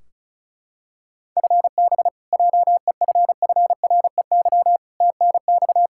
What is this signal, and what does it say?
Morse code, heard as a keyed tone, about 36 wpm, 700 Hz. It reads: FB JEFFREY TNX
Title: K1USN SST QSO Practice 35wpm